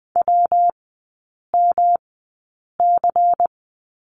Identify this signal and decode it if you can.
Morse code, heard as a keyed tone, about 20 wpm, spaced as for 8 wpm, 700 Hz. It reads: WMC
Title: 114 Lesson 19e - Letter w - Three Characters 20wpm